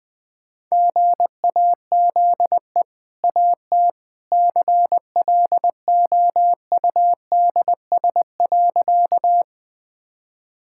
Morse code, heard as a keyed tone, about 21 wpm, 700 Hz. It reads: GAZE AT CLOUDS.
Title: Easy Sentences with Hints - 20wpm